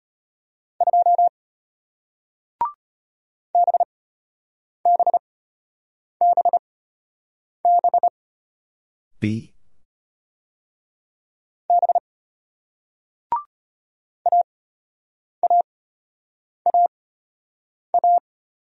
Morse code, heard as a keyed tone, about 36 wpm, spaced as for 8 wpm, 700 Hz. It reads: 2 BBBB B AAAA